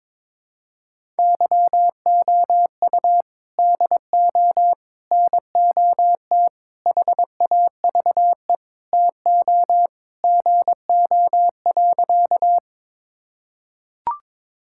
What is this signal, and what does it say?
Morse code, heard as a keyed tone, about 22 wpm, 700 Hz. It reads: YOU DO NOT HAVE TO GO.
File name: practice / 22wpm / Sentences with 3-Word Phrases - 22wpm